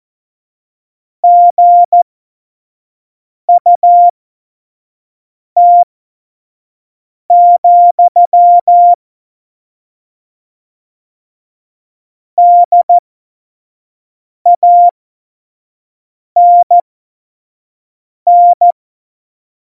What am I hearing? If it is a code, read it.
Morse code, heard as a keyed tone, about 14 wpm, spaced as for 5 wpm, 700 Hz. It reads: GUT, DANN
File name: Morsekurs / Witze / Witz 12 Mai 2020, 25 F70